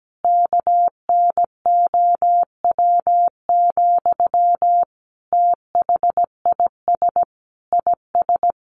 Morse code, heard as a keyed tone, about 17 wpm, 700 Hz. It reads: KNOW, THIS IS
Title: Sentences from Top 1100 Words 17wpm